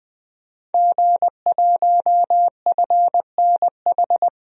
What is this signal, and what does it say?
Morse code, heard as a keyed tone, about 20 wpm, 700 Hz. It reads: G1FNH
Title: DXCC-100 Call Signs - 5 Characters - 20wpm